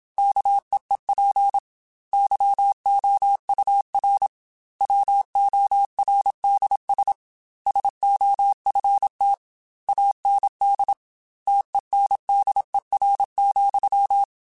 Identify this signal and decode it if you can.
Morse code, heard as a keyed tone, about 27 wpm, 800 Hz. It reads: KEEP YOUR WORDS SOFT AND TENDER,
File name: Chuckles / Chuckles-mp3s / SoftWords @25